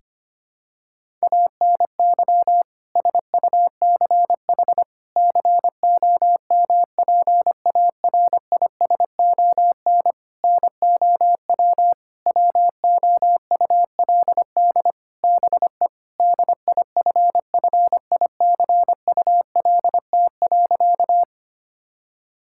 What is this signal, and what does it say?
Morse code, heard as a keyed tone, about 25 wpm, 700 Hz. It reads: ANY SUCH COMPARISON NOW WOULD BE DIFFICULT.
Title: Sentences from Top 1600 Words 25wpm